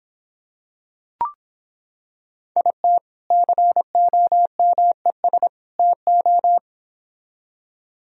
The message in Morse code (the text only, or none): IT COMES TO